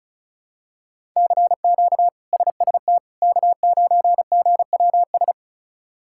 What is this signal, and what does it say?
Morse code, heard as a keyed tone, about 36 wpm, 700 Hz. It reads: CQ SST K9GWS
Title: K1USN SST QSO Practice - 35wpm